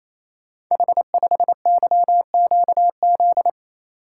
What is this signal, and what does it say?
Morse code, heard as a keyed tone, about 27 wpm, 700 Hz. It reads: H5YQZ
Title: Code Groups - 5 Letter:Numbers - 28wpm